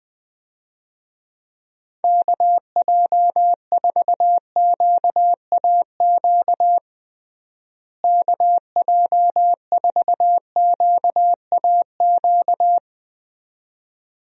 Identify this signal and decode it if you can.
Morse code, heard as a keyed tone, about 20 wpm, 700 Hz. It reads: KJ4QAQ KJ4QAQ
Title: USA Call Signs - 6 Characters - Repeated 2x - 20wpm